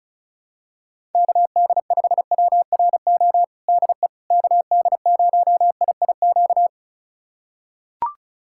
Morse code, heard as a keyed tone, about 34 wpm, 700 Hz. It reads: KD5WRO DE KD0IIQ
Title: USA Callsign DE Callsign - 6 Characters - 35wpm